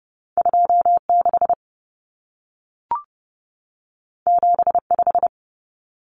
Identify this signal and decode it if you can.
Morse code, heard as a keyed tone, about 30 wpm, 700 Hz. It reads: 26 75
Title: Numbers - 2-Digit 30wpm